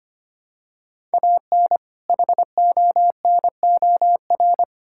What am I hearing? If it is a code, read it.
Morse code, heard as a keyed tone, about 25 wpm, 700 Hz. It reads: AN HONOR
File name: practice / 25wpm / All-Age Appropriate Jokes - 25wpm